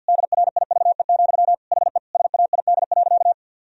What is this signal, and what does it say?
Morse code, heard as a keyed tone, about 50 wpm, 700 Hz. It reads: DRIVE, HE SAID.
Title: Sentences from Top 1000 Words - Review Part 2 - 50wpm